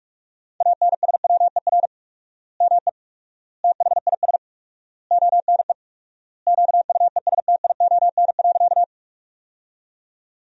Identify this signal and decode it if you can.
Morse code, heard as a keyed tone, about 41 wpm, 700 Hz. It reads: ANSWER ME THIS ONE QUESTION.
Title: 4x Word-Spacing - Sentences from Top 400 Words - 45wpm